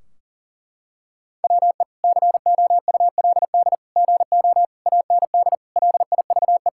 Morse code, heard as a keyed tone, about 40 wpm, 700 Hz. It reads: WE COULD GO AND LIVE
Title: Sentences from Top 1600 Words 40wpm